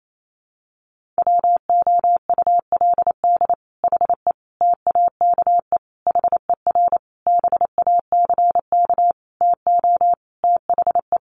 Morse code, heard as a keyed tone, about 28 wpm, 700 Hz. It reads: WOULD HE TAKE HER BACK TO THE
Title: Sentences from Top 1500 Words 28wpm